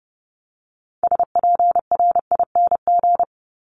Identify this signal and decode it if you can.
Morse code, heard as a keyed tone, about 30 wpm, 700 Hz. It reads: SPRING
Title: Top 1001-1250 Words 30wpm